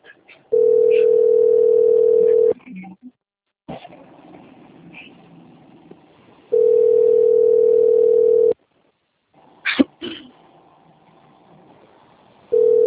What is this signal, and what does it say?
Ringback tone (North America), 440 + 480 Hz, with speech.
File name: rg-1600-unknown- - - .69145